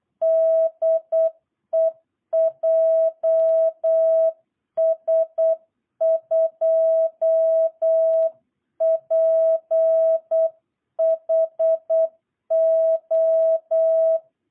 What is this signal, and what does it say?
Morse code, heard as a keyed tone, about 8 wpm, 660 Hz. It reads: DE JS2PHO
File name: jh8cbh / 2022cw